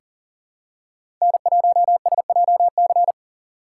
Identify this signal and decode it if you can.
Morse code, heard as a keyed tone, about 40 wpm, 700 Hz. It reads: N1SJC
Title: USA Call Signs - 5 Characters - 4x Character Spacing - 40wpm